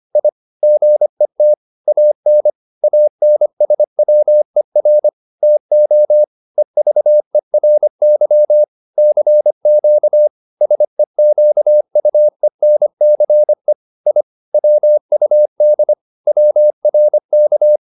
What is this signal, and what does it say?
Morse code, heard as a keyed tone, about 25 wpm, 600 Hz. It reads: I GET AN ANSWER TO EVERY CQ SEQUENCE I WUD WRK